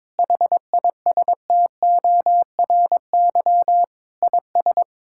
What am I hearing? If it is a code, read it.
Morse code, heard as a keyed tone, about 22 wpm, 700 Hz. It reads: HISTORY IS
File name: practice / 22wpm / Sentences with 8-Word Phrases - 22wpm